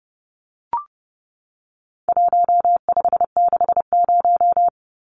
Morse code, heard as a keyed tone, about 30 wpm, 700 Hz. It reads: 1560
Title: Numbers - Contest Style - 4-Digit 30wpm